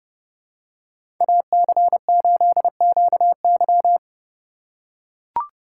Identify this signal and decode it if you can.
Morse code, heard as a keyed tone, about 32 wpm, 700 Hz. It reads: AC8QY
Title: USA Call Signs - 5 Characters - 2x Character Spacing - 30wpm